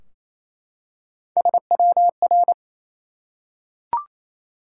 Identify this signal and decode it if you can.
Morse code, heard as a keyed tone, about 29 wpm, 700 Hz. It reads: SWR
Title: Top 180 QSO Elements 28wpm